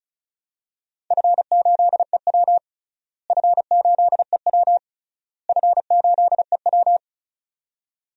Morse code, heard as a keyed tone, about 36 wpm, 700 Hz. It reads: F8EW F8EW F8EW
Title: DXCC-100 Call Signs - 4 Characters - Repeated 3x - 35wpm